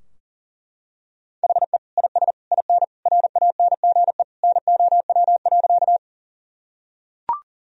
Morse code, heard as a keyed tone, about 40 wpm, 700 Hz. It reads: HE IS IN RANGE NOW.
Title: Sentences from Top 1000 Words - Review Part 4 - 40wpm